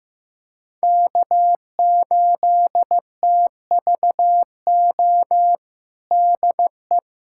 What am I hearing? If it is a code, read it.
Morse code, heard as a keyed tone, about 15 wpm, 700 Hz. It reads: K8TVO DE
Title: USA Callsign DE Callsign - 5 Characters - 15wpm